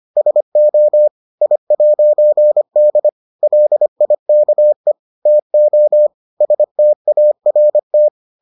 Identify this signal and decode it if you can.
Morse code, heard as a keyed tone, about 25 wpm, 600 Hz. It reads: SO I'D LIKE TO START